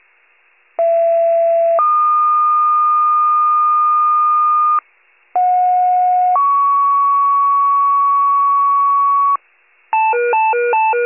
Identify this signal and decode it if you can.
Two-tone page, heard as a one-second tone, then a three-second tone, about 670 then 1220 Hz, about 730 then 1120 Hz. It is heard from 0.8 to 9.4 s.